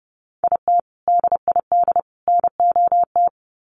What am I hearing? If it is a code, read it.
Morse code, heard as a keyed tone, about 30 wpm, 700 Hz. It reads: IT DID NOT
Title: Sets of 3 Words 30wpm